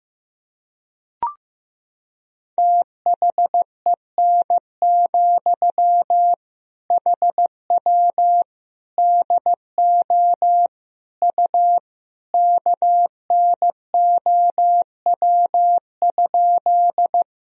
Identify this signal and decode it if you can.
Morse code, heard as a keyed tone, about 15 wpm, 700 Hz. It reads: THEN, HW DO U KNOW?
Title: Sentences from Top 100 Words and Abbreviations 15wpm